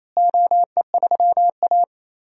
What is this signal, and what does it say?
Morse code, heard as a keyed tone, about 27 wpm, 700 Hz. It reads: OE3A
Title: DXCC-100 Call Signs - 4 Characters - Repeated 3x - 28wpm